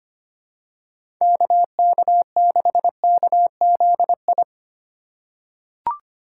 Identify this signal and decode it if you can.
Morse code, heard as a keyed tone, about 25 wpm, 700 Hz. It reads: KK6KZI
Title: USA Call Signs - 6 Characters - 2x Character Spacing - 25wpm